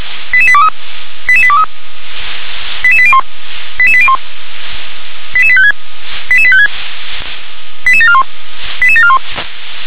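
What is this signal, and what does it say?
Five-tone selective call (ZVEI), heard as quick runs of five tones, tones of about 70 ms. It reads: 88823 88823 88801 88801 88856 88856 88741 88741
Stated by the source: ZVEI five-tone selective call received by SDR.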